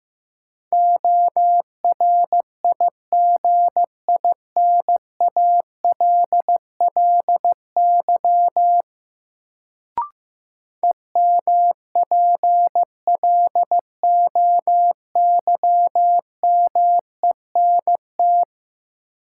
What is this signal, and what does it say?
Morse code, heard as a keyed tone, about 15 wpm, 700 Hz. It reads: ORIGINALLY EMPLOYMENT
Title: Sentences from Top 1500 Words - New Word Drill Alt - 15wpm